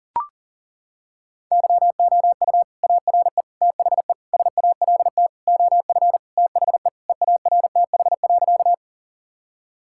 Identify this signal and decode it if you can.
Morse code, heard as a keyed tone, about 40 wpm, 700 Hz. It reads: YOU ARE THE SALT OF THE EARTH.
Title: Sentences from Top 1000 Words - Review Part 5 - 40wpm